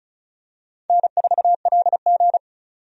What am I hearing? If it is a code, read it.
Morse code, heard as a keyed tone, about 34 wpm, 700 Hz. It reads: N4LG
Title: USA Call Signs - 4 Characters - 35wpm